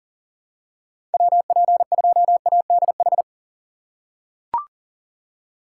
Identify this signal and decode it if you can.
Morse code, heard as a keyed tone, about 40 wpm, 700 Hz. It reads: WP2ADH
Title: USA Call Signs - 6 Characters - 2x Character Spacing - 40wpm